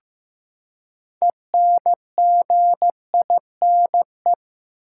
Morse code, heard as a keyed tone, about 15 wpm, 700 Hz. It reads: ENGINE
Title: Six Letter Words - 15wpm